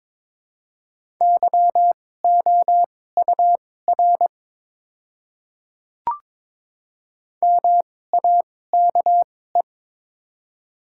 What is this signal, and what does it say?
Morse code, heard as a keyed tone, about 22 wpm, spaced as for 16 wpm, 690 Hz. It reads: YOUR MAKE
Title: 2x Character Spacing - Top 100 Words - 22wpm